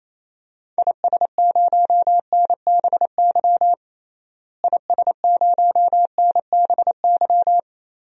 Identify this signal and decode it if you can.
Morse code, heard as a keyed tone, about 28 wpm, 700 Hz. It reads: IS0NBY IS0NBY
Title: DXCC-100 Call Signs - 6 Characters - Repeated 3x - 28wpm